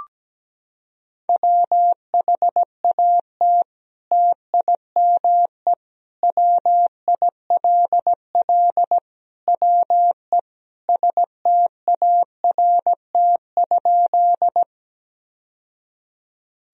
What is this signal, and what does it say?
Morse code, heard as a keyed tone, about 17 wpm, 700 Hz. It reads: WHAT TIME WILL WE START?
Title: Sentences from Top 1000 Words - Review Part 1 - 17wpm